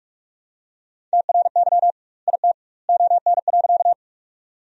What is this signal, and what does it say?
Morse code, heard as a keyed tone, about 46 wpm, 700 Hz. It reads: TRY IT ON.
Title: 2x Word-Spacing - Sentences from Top 200 Words - 45wpm